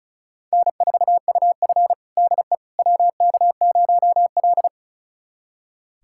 Morse code, heard as a keyed tone, about 36 wpm, 700 Hz. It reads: N4UF DE WK0L
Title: USA Callsign DE Callsign - 4 Characters - 35wpm